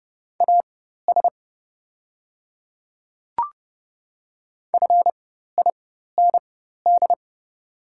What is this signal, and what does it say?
Morse code, heard as a keyed tone, about 29 wpm, spaced as for 14 wpm, 700 Hz. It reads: AS FIND
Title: 4x Character Spacing - Top 200 Words - 30wpm